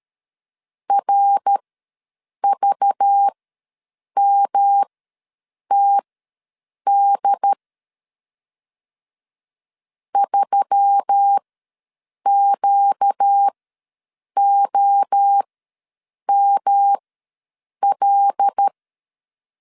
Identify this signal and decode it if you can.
Morse code, heard as a keyed tone, about 13 wpm, spaced as for 7 wpm, 800 Hz. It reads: RVMTD 3QOML
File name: LEZIONI / 5 LBVC / 5 LBV ws 12wpm 3015